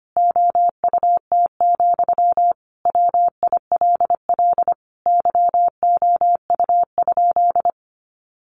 Morse code, heard as a keyed tone, about 25 wpm, 700 Hz. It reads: OUT, WILL YOU?
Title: Sentences from Top 300 Words 25wpm